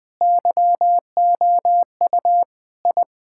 Morse code, heard as a keyed tone, about 20 wpm, 700 Hz. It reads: YOU I
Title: Sentences with 5-Word Phrases - 20wpm